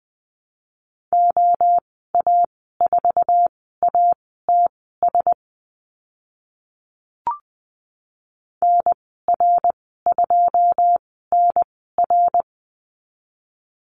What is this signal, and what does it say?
Morse code, heard as a keyed tone, about 20 wpm, spaced as for 14 wpm, 700 Hz. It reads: OA4ATS NR2NR
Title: 082 Lesson 14g - Number 2 - Call Signs 20wpm